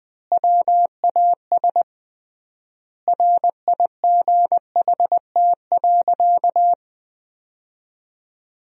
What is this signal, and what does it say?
Morse code, heard as a keyed tone, about 21 wpm, 700 Hz. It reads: WAS RIGHT.
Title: 3x Word-Spacing - Sentences from Top 500 Words - 20wpm